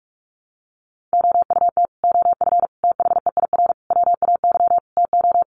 Morse code, heard as a keyed tone, about 44 wpm, 700 Hz. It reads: OUT OF THEIR WAY TO
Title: Sets of 5 Words Encore2 45wpm